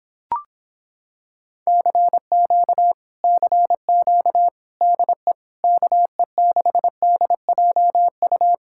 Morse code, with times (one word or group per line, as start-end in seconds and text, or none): 1.7-2.9 CQ
3.2-4.5 CQ
4.8-5.3 DE
5.6-8.5 KE6DJU